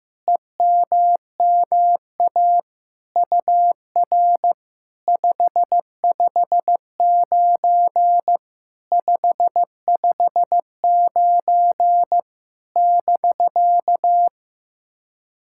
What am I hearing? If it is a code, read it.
Morse code, heard as a keyed tone, about 15 wpm, 700 Hz. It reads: EMMA UR 559 559 <BK>
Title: SOTA QSO Practice 15wpm